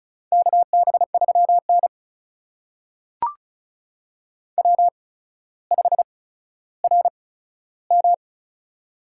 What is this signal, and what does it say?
Morse code, heard as a keyed tone, about 36 wpm, spaced as for 15 wpm, 700 Hz. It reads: KB3N W5RM
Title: USA Call Signs - 4 Characters - 8x Character Spacing - 35wpm